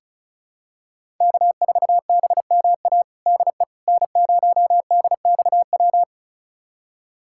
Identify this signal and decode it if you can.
Morse code, heard as a keyed tone, about 36 wpm, 700 Hz. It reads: K4BMA DE N0DXW
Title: USA Callsign DE Callsign - 5 Characters - 35wpm